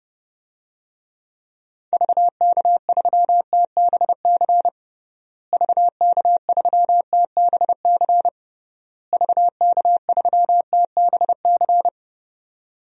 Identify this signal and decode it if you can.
Morse code, heard as a keyed tone, about 32 wpm, 700 Hz. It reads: VK3TBC VK3TBC VK3TBC
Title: DXCC-100 Call Signs - 6 Characters - Repeated 3x - 30wpm